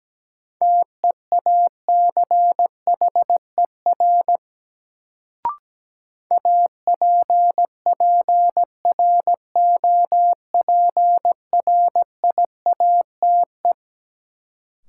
Morse code, heard as a keyed tone, about 17 wpm, 700 Hz. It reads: TEACHER APPROPRIATE
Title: Sentences from Top 900 Words - New Word Drill - 17wpm